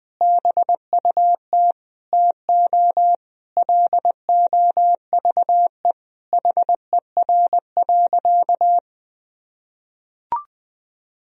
Morse code, heard as a keyed tone, about 21 wpm, 700 Hz. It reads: BUT TO LOVE HER.
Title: Sentences from Top 1000 Words - Review Part 4 - 20wpm